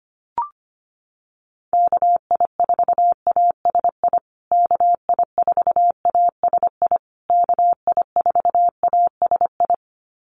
Morse code, heard as a keyed tone, about 25 wpm, 700 Hz. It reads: KI4ASI KI4ASI KI4ASI
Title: USA Call Signs Repeated 3x 25wpm